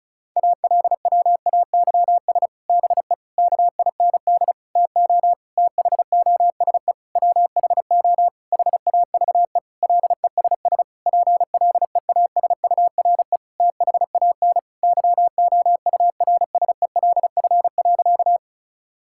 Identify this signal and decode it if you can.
Morse code, heard as a keyed tone, about 36 wpm, 700 Hz. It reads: ALWAYS BE KIND TO THOSE WHO HAVE LESS PLEASURE THAN YOURSELF.
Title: Sentences from Top 1600 Words 35wpm